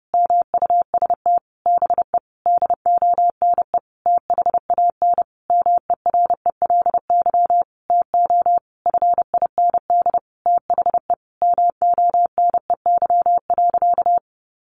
Morse code, heard as a keyed tone, about 28 wpm, 700 Hz. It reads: MUST BE DONE THAN MERELY TO FIND THE MONEY.
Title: Sentences from Top 1400 Words 30wpm